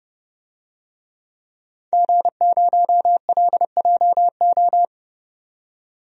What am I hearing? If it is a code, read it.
Morse code, heard as a keyed tone, about 30 wpm, 700 Hz. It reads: G0LJO
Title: DXCC-100 Call Signs - 5 Characters - 30wpm